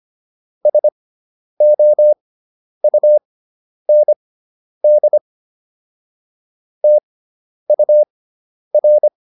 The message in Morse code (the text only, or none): SOUND TUR